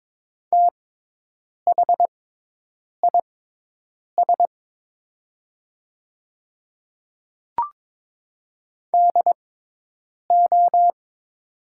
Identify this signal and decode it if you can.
Morse code, heard as a keyed tone, about 22 wpm, spaced as for 8 wpm, 700 Hz. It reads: THIS DO
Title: 6x Character Spacing - Top 100 Words - 22wpm